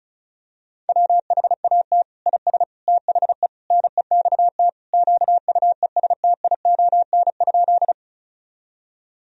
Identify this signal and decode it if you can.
Morse code, heard as a keyed tone, about 36 wpm, 700 Hz. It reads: WHAT IS THE NEXT QUESTION?
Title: Sentences from Top 1000 Words - Review Part 1 - 35wpm